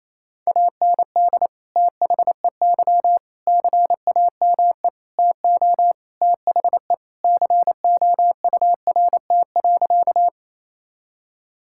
Morse code, heard as a keyed tone, about 28 wpm, 700 Hz. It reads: AND THEY CAME TO THE COURT.
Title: Sentences from Top 1600 Words 28wpm